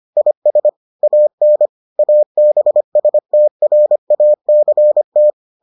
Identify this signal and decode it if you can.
Morse code, heard as a keyed tone, about 25 wpm, 600 Hz. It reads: IS AN ABSTRACT